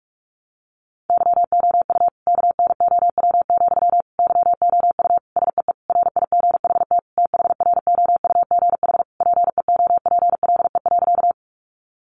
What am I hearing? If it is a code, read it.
Morse code, heard as a keyed tone, about 44 wpm, 700 Hz. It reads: YOU KNOW, YOU SEE RIGHT THROUGH PEOPLE.
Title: Sentences from Top 200 Words 45wpm